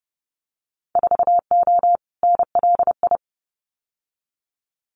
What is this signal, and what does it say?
Morse code, heard as a keyed tone, about 30 wpm, 700 Hz. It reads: 4O NLI
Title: Winter Field Day Exchanges Encore 30wpm